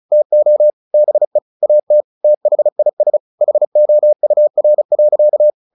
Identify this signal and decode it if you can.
Morse code, heard as a keyed tone, about 34 wpm, 600 Hz. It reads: TO BE AT THIS HOUR.
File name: the old man and the sea / the old man and the sea 1 35wpm